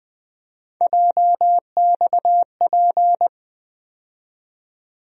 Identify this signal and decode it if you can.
Morse code, heard as a keyed tone, about 20 wpm, 700 Hz. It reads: JXP
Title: USA Call Sign Postfixes - 3 Letters 20wpm